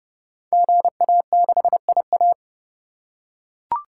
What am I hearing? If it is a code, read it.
Morse code, heard as a keyed tone, about 29 wpm, 700 Hz. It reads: GA6IA
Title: Code Groups - 5 Letter:Numbers - 30wpm